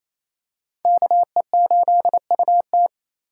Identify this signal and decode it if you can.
Morse code, heard as a keyed tone, about 28 wpm, 700 Hz. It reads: KE8UT
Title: USA Call Signs - 4x Character Spacing - 28wpm